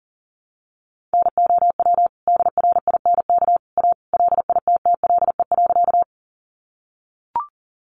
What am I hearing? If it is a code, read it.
Morse code, heard as a keyed tone, about 40 wpm, 700 Hz. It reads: NOW DRINK A LITTLE.
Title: Sentences from Top 1500 Words 40wpm